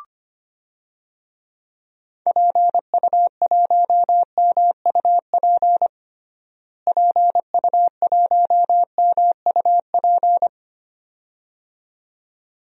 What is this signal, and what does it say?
Morse code, heard as a keyed tone, about 25 wpm, 700 Hz. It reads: PU1MUP PU1MUP
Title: DXCC-100 Call Signs - 6 Characters - Repeated 2x - 25wpm